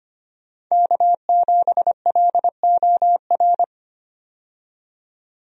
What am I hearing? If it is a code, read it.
Morse code, heard as a keyed tone, about 25 wpm, 700 Hz. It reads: K7LOR
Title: USA Call Signs - 5 Characters - 25wpm